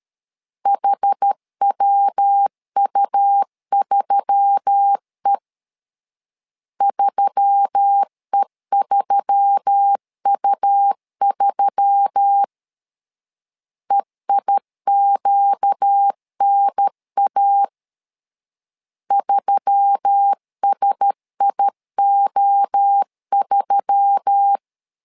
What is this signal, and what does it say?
Morse code, heard as a keyed tone, about 13 wpm, 800 Hz. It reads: HWU3E 3E3U3 EIQNA 3SIO3
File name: LEZIONI / 4 Q1S3 12wpm 2023